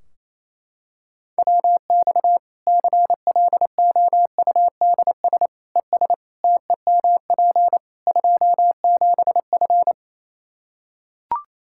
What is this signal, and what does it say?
Morse code, heard as a keyed tone, about 28 wpm, 700 Hz. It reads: WX CLOUDS ES TEMP 27F
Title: Practice QSO - Exchange 2 - Info - 28wpm